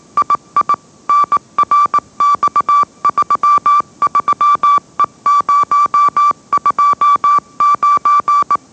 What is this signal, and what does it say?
Morse code, heard as a keyed tone, about 18 wpm, 1160 Hz. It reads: II NRX33E029
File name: mrx / mrx081 20wpm